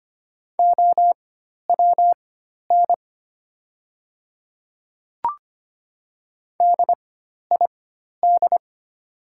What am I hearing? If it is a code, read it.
Morse code, heard as a keyed tone, about 25 wpm, spaced as for 12 wpm, 690 Hz. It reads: OWN DID
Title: 4x Character Spacing - Top 300 Words - 25wpm